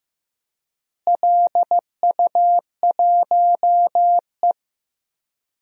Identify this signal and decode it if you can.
Morse code, heard as a keyed tone, about 15 wpm, 700 Hz. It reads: LU1E
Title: DXCC-100 Call Signs - 4 Characters - Repeated 3x - 15wpm